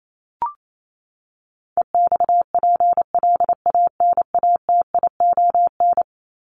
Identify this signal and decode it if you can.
Morse code, heard as a keyed tone, about 28 wpm, 700 Hz. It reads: EXPLANATION
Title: Top 1751-2000 Words 28wpm